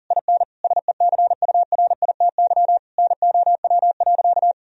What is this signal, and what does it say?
Morse code, heard as a keyed tone, about 40 wpm, 700 Hz. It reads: IN SECURITY NOW.
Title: Sentences from Top 1000 Words - Review Part 4 - 40wpm